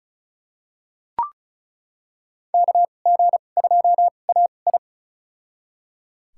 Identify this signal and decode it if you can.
Morse code, heard as a keyed tone, about 36 wpm, spaced as for 25 wpm, 700 Hz. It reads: KG2AI
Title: USA Call Signs - 5 Characters - 2x Character Spacing - 35wpm